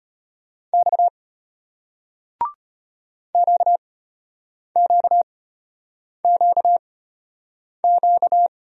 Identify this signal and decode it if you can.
Morse code, heard as a keyed tone, about 33 wpm, spaced as for 8 wpm, 700 Hz. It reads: X QQQQ